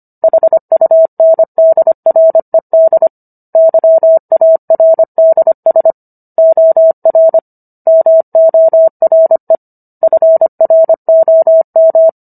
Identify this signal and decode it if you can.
Morse code, heard as a keyed tone, about 25 wpm, 650 Hz. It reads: HUNDRED YARDS OR MORE FROM